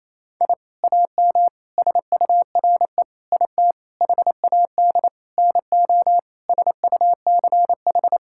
Morse code, heard as a keyed tone, about 27 wpm, 700 Hz. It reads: I AM SURE IT HAD NO SUCH
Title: Sentences from Top 1000 Words - Review Part 6 - 28wpm